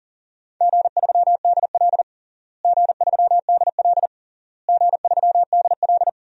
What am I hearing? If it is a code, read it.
Morse code, heard as a keyed tone, about 37 wpm, 700 Hz. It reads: G3DL G3DL G3DL
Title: DXCC-100 Call Signs - 4 Characters - Repeated 3x - 40wpm